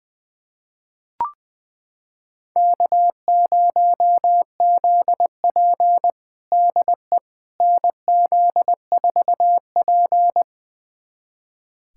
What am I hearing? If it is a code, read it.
Morse code, heard as a keyed tone, about 20 wpm, 700 Hz. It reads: K0ZP DE NZ4P
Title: USA Callsign DE Callsign - 4 Characters - 20wpm